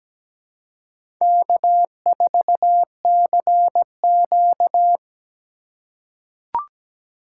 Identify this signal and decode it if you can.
Morse code, heard as a keyed tone, about 17 wpm, 700 Hz. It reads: K4CQ
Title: USA Call Signs - 4 Characters - 2x Character Spacing - 17wpm